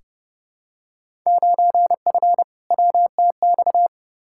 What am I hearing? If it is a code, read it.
Morse code, heard as a keyed tone, about 32 wpm, 700 Hz. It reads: 9F WTX
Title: Field Day Exchanges Encore - 30wpm